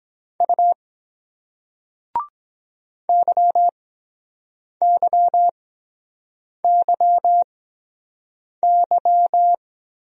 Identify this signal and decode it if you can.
Morse code, heard as a keyed tone, about 22 wpm, spaced as for 7 wpm, 700 Hz. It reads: U YYYY